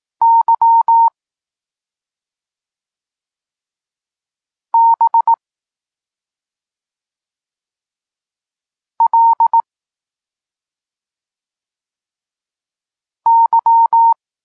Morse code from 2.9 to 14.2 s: B L Y